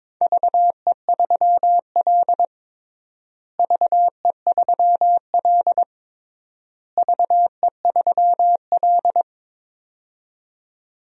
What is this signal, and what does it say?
Morse code, heard as a keyed tone, about 21 wpm, 700 Hz. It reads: VE3L VE3L VE3L
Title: DXCC-100 Call Signs - 4 Characters - Repeated 3x - 22wpm